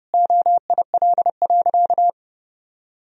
Morse code, heard as a keyed tone, about 30 wpm, 700 Hz. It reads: OIL.
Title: Sentences from Top 1000 Words - Review Part 4 - 30wpm